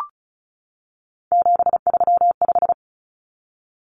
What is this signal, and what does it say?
Morse code, heard as a keyed tone, about 34 wpm, 700 Hz. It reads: 735
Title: Common Cut Numbers - 3-Digit 35wpm